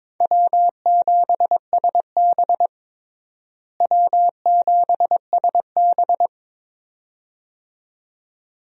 Morse code, heard as a keyed tone, about 22 wpm, 700 Hz. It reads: W7SB W7SB
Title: USA Call Signs - 4 Characters - Repeated 2x - 22wpm